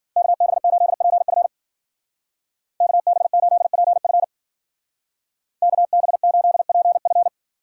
Morse code, heard as a keyed tone, about 48 wpm, 700 Hz. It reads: KB8PF KB8PF KB8PF
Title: Speed Racing - USA 2x2 Call Signs 40wpm